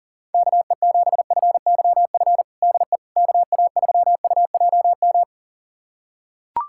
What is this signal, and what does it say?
Morse code, heard as a keyed tone, about 40 wpm, 700 Hz. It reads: KE7FYF DE KA3UJM
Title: USA Callsign DE Callsign - 6 Characters - 40wpm